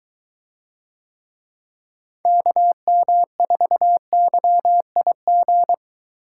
Morse code, heard as a keyed tone, about 23 wpm, 700 Hz. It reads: KM4YIG